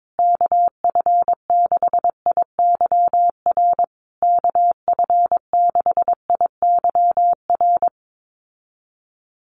Morse code, heard as a keyed tone, about 22 wpm, 700 Hz. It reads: KF6IYR KF6IYR
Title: USA Call Signs Repeated 3x 22wpm